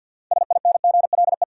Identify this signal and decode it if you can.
Morse code, heard as a keyed tone, about 49 wpm, 700 Hz. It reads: SINGLE
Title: Six Letter Words - 50wpm